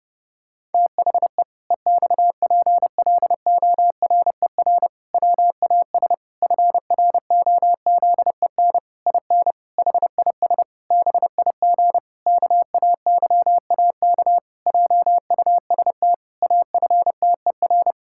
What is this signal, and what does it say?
Morse code, heard as a keyed tone, about 32 wpm, 700 Hz. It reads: THE EXPLORER WAS FROZEN IN HIS BIG KAYAK JUST AFTER
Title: Pangram Challenge v2 - 30wpm